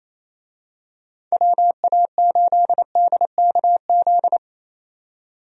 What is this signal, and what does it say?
Morse code, heard as a keyed tone, about 27 wpm, 700 Hz. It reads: WA8DKZ
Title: USA Call Signs - 6 Characters - 28wpm